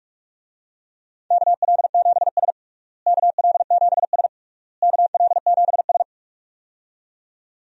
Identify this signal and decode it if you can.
Morse code, heard as a keyed tone, about 46 wpm, 700 Hz. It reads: KL7S KL7S KL7S
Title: DXCC-100 Call Signs - 4 Characters - Repeated 3x - 45wpm